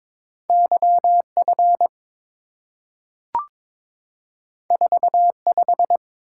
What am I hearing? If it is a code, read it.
Morse code, heard as a keyed tone, about 22 wpm, 700 Hz. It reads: YF 45
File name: practice / 22wpm / Two Letter-Number Combinations - 22wpm